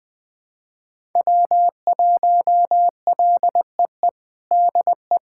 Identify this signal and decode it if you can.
Morse code, heard as a keyed tone, about 20 wpm, 700 Hz. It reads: W1LEE DE
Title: USA Callsign DE Callsign - 5 Characters - 20wpm